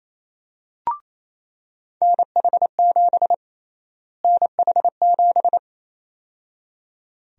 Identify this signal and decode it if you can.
Morse code, heard as a keyed tone, about 28 wpm, 700 Hz. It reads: NH7 NH7
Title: USA Call Sign Prefixes - Repeated 2x - 28wpm